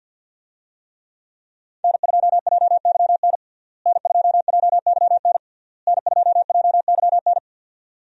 Morse code, heard as a keyed tone, about 52 wpm, 700 Hz. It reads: N2JYN N2JYN N2JYN
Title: USA Call Signs - 5 Characters - Repeated 3x - 50wpm